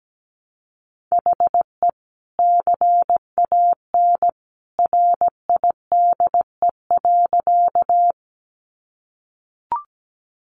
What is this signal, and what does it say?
Morse code, heard as a keyed tone, about 17 wpm, 700 Hz. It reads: HE CAN RIDE.
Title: Sentences from Top 1200 Words 17wpm